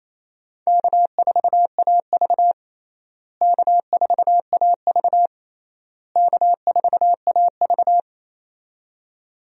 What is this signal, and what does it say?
Morse code, heard as a keyed tone, about 27 wpm, 700 Hz. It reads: K4AV K4AV K4AV
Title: USA Call Signs - 4 Characters - Repeated 3x - 28wpm